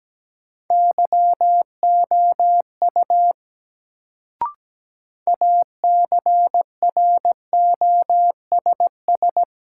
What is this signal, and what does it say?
Morse code, heard as a keyed tone, about 17 wpm, 700 Hz. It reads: YOU ACROSS
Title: Sets of 2 Words - New Word Drill - 17wpm